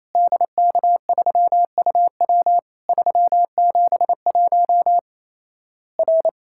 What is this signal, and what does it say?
Morse code, heard as a keyed tone, about 28 wpm, 700 Hz. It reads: DK3UW 371 R
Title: Worked All Europe DX Contest - QTC Exchange - 28wpm